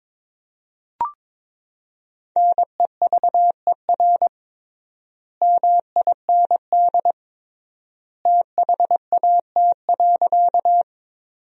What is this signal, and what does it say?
Morse code, heard as a keyed tone, about 22 wpm, 700 Hz. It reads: NEVER MIND THAT.
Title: 3x Word-Spacing - Sentences from Top 500 Words - 22wpm